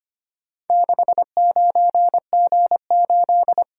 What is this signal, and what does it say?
Morse code, heard as a keyed tone, about 25 wpm, 700 Hz. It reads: 69G8
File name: practice / 25wpm / Four Letter-Number Combinations - 25wpm